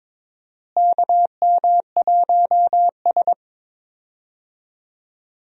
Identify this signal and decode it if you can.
Morse code, heard as a keyed tone, about 22 wpm, 700 Hz. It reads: KM1S
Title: USA Call Signs - 4 Characters - Repeated 3x - 22wpm